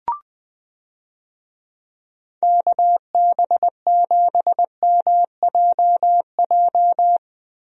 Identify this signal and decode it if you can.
Morse code, heard as a keyed tone, about 20 wpm, 700 Hz. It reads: KB7MJJ
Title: USA Call Signs - 6 Characters - Repeated 3x - 20wpm